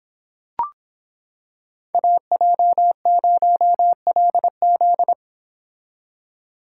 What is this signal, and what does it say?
Morse code, heard as a keyed tone, about 26 wpm, 700 Hz. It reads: AJ0LZ